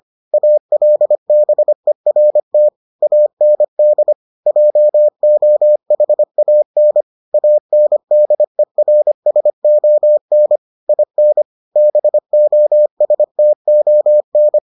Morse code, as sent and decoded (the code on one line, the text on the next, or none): .- .-.. -... .-. - / .- -. -.. / .--- --- .... .- -. / .- -. -.. .-. ... --- -. / .. -. / -... --- ... - --- -.
ALBERT AND JOHAN ANDERSON IN BOSTON